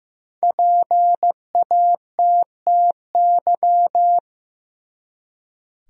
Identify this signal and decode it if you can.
Morse code, heard as a keyed tone, about 15 wpm, 700 Hz. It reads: PATTY
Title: USA Common Names - 15wpm